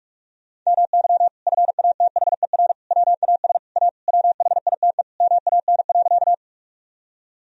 Morse code, heard as a keyed tone, about 41 wpm, 700 Hz. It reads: MY FATHER WAS A WHITE MAN.
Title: Sentences from Top 1000 Words - Review Part 1 - 45wpm